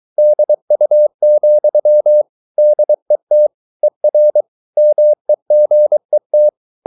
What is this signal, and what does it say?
Morse code, heard as a keyed tone, about 23 wpm, 600 Hz. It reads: DU, DET ER MEGET
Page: Ferie 23 wpm